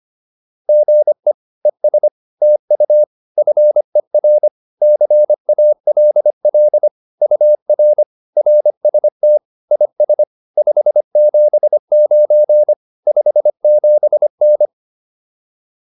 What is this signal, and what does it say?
Morse code, heard as a keyed tone, about 25 wpm, 600 Hz. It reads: GE ES TU FER CALL UR RST IS 579 57N